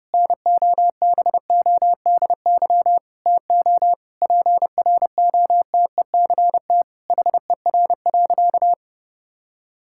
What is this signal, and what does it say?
Morse code, heard as a keyed tone, about 30 wpm, 700 Hz. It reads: NOBODY TO PROTECT HER.
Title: Sentences from Top 1600 Words 30wpm